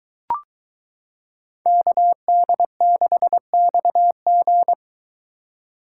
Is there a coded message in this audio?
Yes: Morse code from 1.7 to 4.7 s, heard as a keyed tone, about 23 wpm, 700 Hz. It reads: KD6XG